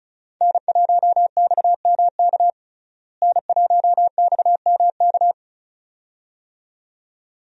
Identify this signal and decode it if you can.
Morse code, heard as a keyed tone, about 34 wpm, 700 Hz. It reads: N1XMK N1XMK
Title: USA Call Signs - 5 Characters - Repeated 2x - 35wpm